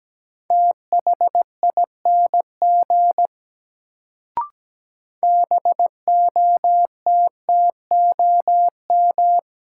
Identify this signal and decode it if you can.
Morse code, heard as a keyed tone, about 17 wpm, 700 Hz. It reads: THING BOTTOM
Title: Sets of 3 Words - New Word Drill - 17wpm